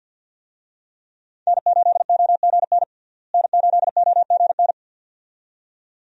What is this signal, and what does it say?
Morse code, heard as a keyed tone, about 47 wpm, 700 Hz. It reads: N8OGN N8OGN
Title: USA Call Signs - 5 Characters - Repeated 2x - 50wpm